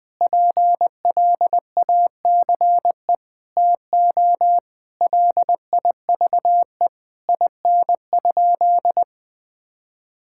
Morse code, heard as a keyed tone, about 19 wpm, 700 Hz. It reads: PLACE TO LIVE IN?
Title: Sentences from Top 1000 Words - Review Part 5 - 20wpm